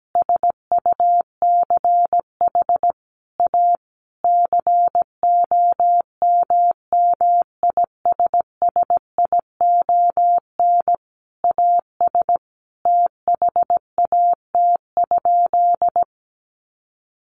Morse code, heard as a keyed tone, about 17 wpm, 700 Hz. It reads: SUCH A COMMISSION AS THAT?
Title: Sentences from Top 1400 Words 17wpm